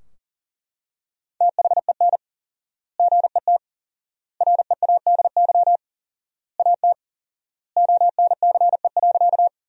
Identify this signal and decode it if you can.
Morse code, heard as a keyed tone, about 40 wpm, 700 Hz. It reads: THEN GET READY AT ONCE.
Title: 4x Word-Spacing - Sentences from Top 400 Words - 40wpm